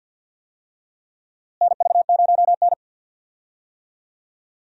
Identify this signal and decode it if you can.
Morse code, heard as a keyed tone, about 47 wpm, 700 Hz. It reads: NV0N
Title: USA Call Signs - 4 Characters - Repeated 3x - 50wpm